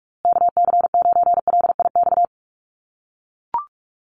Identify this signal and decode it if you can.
Morse code, heard as a keyed tone, about 44 wpm, 700 Hz. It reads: KC9LIX
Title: USA Call Signs Repeated 3x 45wpm